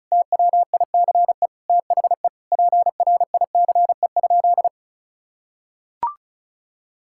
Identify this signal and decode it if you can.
Morse code, heard as a keyed tone, about 36 wpm, 700 Hz. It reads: TWICE THE PRICE?
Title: Sentences from Top 1600 Words 35wpm